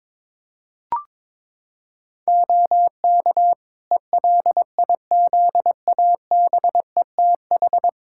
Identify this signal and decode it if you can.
Morse code, heard as a keyed tone, about 22 wpm, 700 Hz. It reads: OK ELIZABETH
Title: Practice QSO - Exchange 3 - More Info - 22wpm